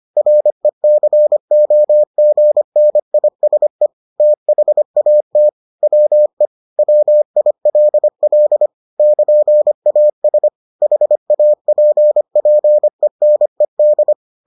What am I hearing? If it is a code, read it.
Morse code, heard as a keyed tone, about 25 wpm, 600 Hz. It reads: RECOGNISE THAT WE WILL <KN>AS HAPPENED